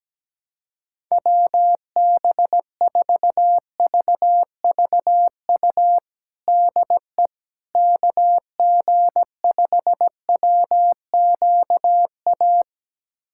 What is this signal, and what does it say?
Morse code, heard as a keyed tone, about 17 wpm, 700 Hz. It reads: WB4VVU DE KG5WQA
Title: USA Callsign DE Callsign - 6 Characters - 17wpm